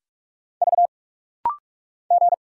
Morse code, heard as a keyed tone, about 46 wpm, 700 Hz. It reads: V G
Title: Single Letter-Number Alt - Rapid-Fire 45wpm